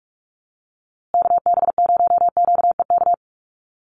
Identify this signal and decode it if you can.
Morse code, heard as a keyed tone, about 44 wpm, 700 Hz. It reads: KB0QEK
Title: USA Call Signs Repeated 2x 45wpm